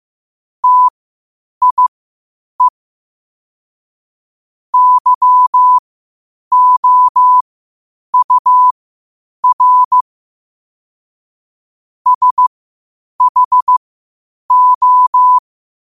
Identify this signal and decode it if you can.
Morse code, heard as a keyed tone, about 15 wpm, spaced as for 8 wpm, 1000 Hz. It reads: TIE YOUR SHO
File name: qotdfiles / 8wpm